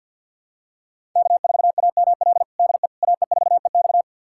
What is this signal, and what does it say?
Morse code, heard as a keyed tone, about 52 wpm, 700 Hz. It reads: K4AML DE AE4EX
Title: USA Callsign DE Callsign - 5 Characters - 50wpm